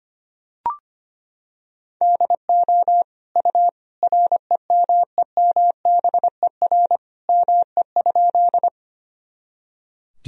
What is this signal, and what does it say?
Morse code, heard as a keyed tone, about 25 wpm, 700 Hz. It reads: DO U REMEMBER ME?
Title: Sentences from Top 400 Words and Abbreviations 25wpm